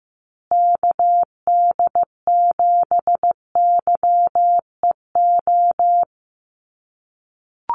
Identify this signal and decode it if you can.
Morse code, heard as a keyed tone, about 15 wpm, 700 Hz. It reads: KD7YEO
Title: USA Call Signs Repeated 2x 15wpm